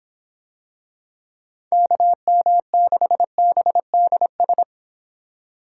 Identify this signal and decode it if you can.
Morse code, heard as a keyed tone, about 26 wpm, 700 Hz. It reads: KM6BDS